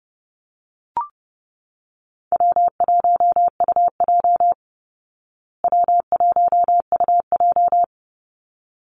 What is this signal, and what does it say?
Morse code, heard as a keyed tone, about 30 wpm, 700 Hz. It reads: W1UJ W1UJ
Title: Call Signs Repeated 3x - 30wpm